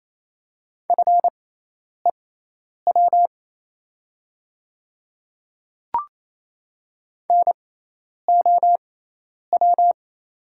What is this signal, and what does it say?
Morse code, heard as a keyed tone, about 27 wpm, spaced as for 10 wpm, 690 Hz. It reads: FEW NOW
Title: 6x Character Spacing - Top 300 Words - 28wpm